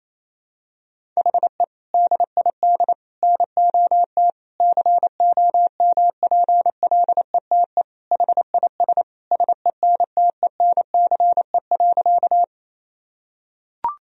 Morse code, heard as a keyed tone, about 27 wpm, 700 Hz. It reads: HE DID NOT COMPLETE HIS SENTENCE.
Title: Sentences from Top 1000 Words - Review Part 1 - 28wpm